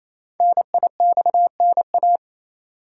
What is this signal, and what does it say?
Morse code, heard as a keyed tone, about 27 wpm, 700 Hz. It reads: NIXNA
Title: Code Groups - 5 Letters - 28wpm